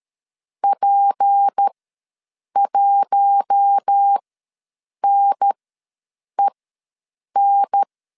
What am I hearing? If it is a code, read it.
Morse code, heard as a keyed tone, about 13 wpm, spaced as for 7 wpm, 800 Hz. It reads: P1NEN
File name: LEZIONI / 6 PKX0 / 6 P ws 12wpm 3117